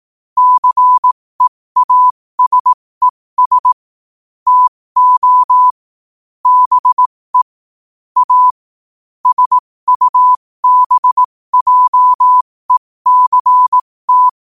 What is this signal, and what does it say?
Morse code, heard as a keyed tone, about 18 wpm, spaced as for 15 wpm, 1000 Hz. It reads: CEASES TO BE A SUBJECT
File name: qotdfiles / 15wpm